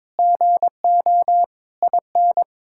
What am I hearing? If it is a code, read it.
Morse code, heard as a keyed tone, about 22 wpm, 700 Hz. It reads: GO IN
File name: Sentences with 9-Word Phrases - 22wpm